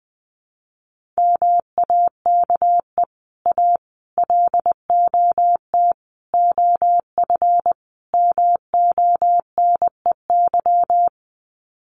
Morse code, heard as a keyed tone, about 20 wpm, 700 Hz. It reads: MAKE A LOT OF MONEY
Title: Sets of 5 Words Encore2 20wpm